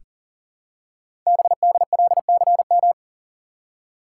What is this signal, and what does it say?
Morse code, heard as a keyed tone, about 44 wpm, 700 Hz. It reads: BDLCM
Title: Code Groups - 5 Letters - 40wpm